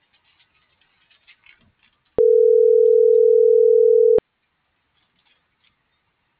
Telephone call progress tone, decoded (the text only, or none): ringback tone